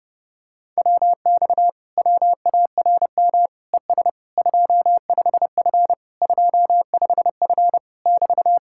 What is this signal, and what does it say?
Morse code, heard as a keyed tone, about 31 wpm, 700 Hz. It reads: WX WARM ES 25F 25F <BT>